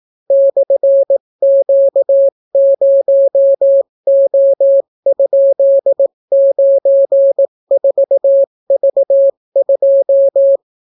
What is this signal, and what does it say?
Morse code, heard as a keyed tone, about 18 wpm, 550 Hz. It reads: /Q0O?94V2